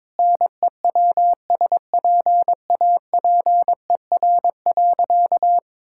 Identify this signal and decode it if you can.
Morse code, heard as a keyed tone, about 22 wpm, 700 Hz. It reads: NEWSPAPER.
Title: Sentences from Top 1000 Words - Review Part 5 - 22wpm